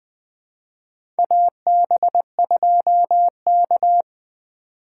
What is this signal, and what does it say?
Morse code, heard as a keyed tone, about 20 wpm, 700 Hz. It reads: AB2K
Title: USA Call Signs - 4 Characters - 2x Character Spacing - 20wpm